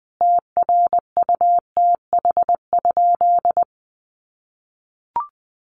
Morse code, heard as a keyed tone, about 20 wpm, 700 Hz. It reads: TRUTH?
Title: Sentences from Top 1100 Words 20wpm